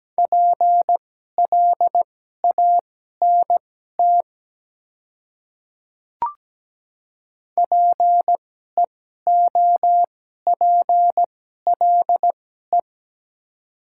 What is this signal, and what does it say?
Morse code, heard as a keyed tone, about 17 wpm, spaced as for 12 wpm, 700 Hz. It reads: PLANT PEOPLE
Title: 2x Character Spacing - Top 300 Words - 17wpm